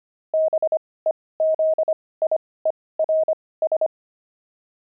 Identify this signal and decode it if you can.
Morse code, heard as a keyed tone, about 25 wpm, spaced as for 18 wpm, 630 Hz. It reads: BEZIERS
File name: pangram-fs-18-cs-25-zaph